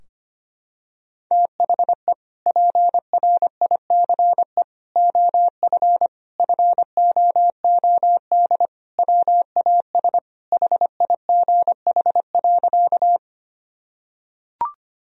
Morse code, heard as a keyed tone, about 25 wpm, 700 Hz. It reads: THE PRICE OF FOOD WAS HIGH.
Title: Sentences from Top 1000 Words - Review Part 4 - 25wpm